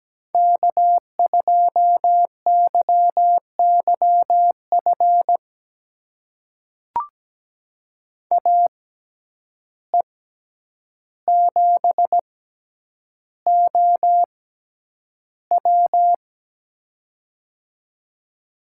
Morse code, heard as a keyed tone, about 17 wpm, spaced as for 9 wpm, 700 Hz. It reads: K2YYF AE7OW
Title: USA Call Signs - 5 Characters - 6x Character Spacing - 17wpm